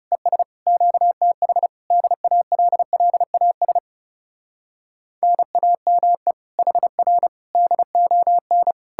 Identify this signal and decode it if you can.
Morse code, heard as a keyed tone, about 40 wpm, 700 Hz. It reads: ES QTH DALLAS NAME HR DON